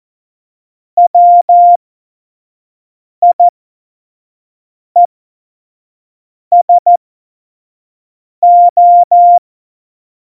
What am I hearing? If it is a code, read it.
Morse code, heard as a keyed tone, about 14 wpm, spaced as for 5 wpm, 700 Hz. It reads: WIESO
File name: Morsekurs / Witze / Witz 01. Oktober 2019, 25 F70